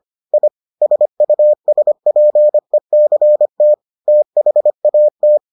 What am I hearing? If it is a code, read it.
Morse code, heard as a keyed tone, about 25 wpm, 600 Hz. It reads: I SUSPECT THAT